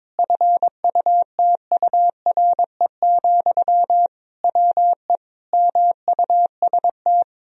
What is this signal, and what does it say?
Morse code, heard as a keyed tone, about 22 wpm, 700 Hz. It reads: FUTURE, WE MUST
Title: Sentences from Top 1000 Words - Review Part 3 - 22wpm